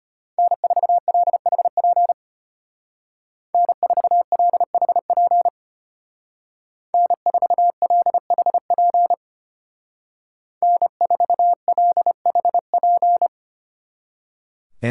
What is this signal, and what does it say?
Morse code, heard as a keyed tone, about 33 wpm, 700 Hz. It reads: N4LHP N4LHP N4LHP N4LHP